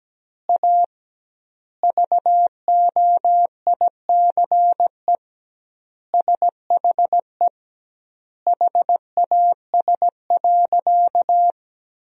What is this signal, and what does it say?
Morse code, heard as a keyed tone, about 17 wpm, 700 Hz. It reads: A VOICE SHE HAS.
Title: 2x Word-Spacing - Sentences from Top 500 Words - 17wpm